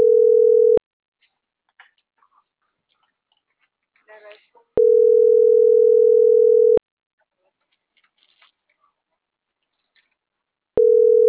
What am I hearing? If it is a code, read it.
Ringback tone (North America), 440 + 480 Hz.